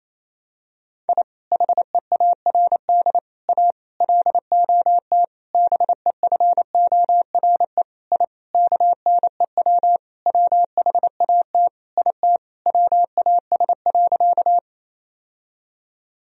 Morse code, heard as a keyed tone, about 28 wpm, 700 Hz. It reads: I HEARD A LOT BEFORE I KNEW WHAT IT WAS.
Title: Sentences from Top 1000 Words - Review Part 3 - 28wpm